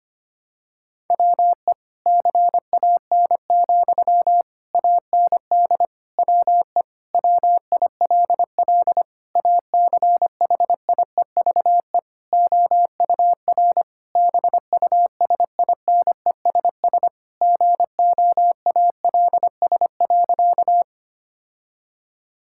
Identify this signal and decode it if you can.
Morse code, heard as a keyed tone, about 25 wpm, 700 Hz. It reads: WE CAN, AND WE WILL ACHIEVE OUR BUSINESS GOALS.
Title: Sentences from Top 1000 Words - Review Part 6 - 25wpm